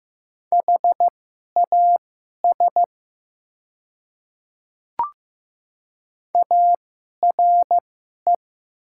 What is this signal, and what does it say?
Morse code, heard as a keyed tone, about 15 wpm, spaced as for 11 wpm, 700 Hz. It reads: HAS ARE
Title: 2x Character Spacing - Top 100 Words - 15wpm